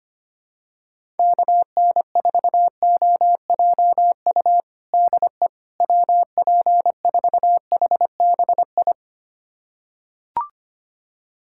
Morse code, heard as a keyed tone, about 25 wpm, 700 Hz. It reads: KN4OJU DE WP4HBI
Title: USA Callsign DE Callsign 25wpm